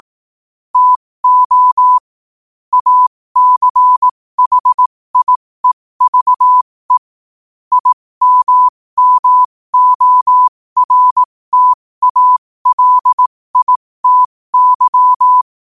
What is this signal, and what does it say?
Morse code, heard as a keyed tone, about 18 wpm, spaced as for 15 wpm, 1000 Hz. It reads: TO ACHIEVE IMMORTALITY
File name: qotdfiles / 15wpm